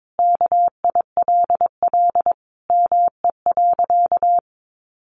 Morse code, heard as a keyed tone, about 22 wpm, 700 Hz. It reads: KILL ME.
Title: Sentences from Top 1400 Words 22wpm